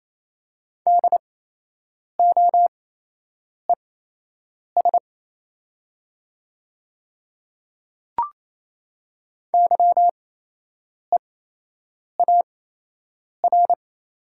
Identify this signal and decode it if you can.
Morse code, heard as a keyed tone, about 28 wpm, spaced as for 8 wpm, 700 Hz. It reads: DOES YEAR
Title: 8x Character Spacing - Top 500 Words - 28wpm